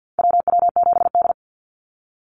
Morse code, heard as a keyed tone, about 49 wpm, 700 Hz. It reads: WW7D
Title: NA SOTA Chaser Callsigns 50wpm